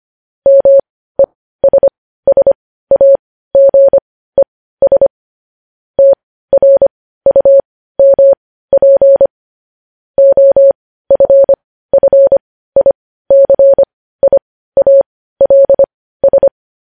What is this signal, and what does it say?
Morse code, heard as a keyed tone, about 25 wpm, spaced as for 15 wpm, 550 Hz. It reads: MESSAGES TRUMP OFFICIALS
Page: News Headlines in Morse Code at 15 WPM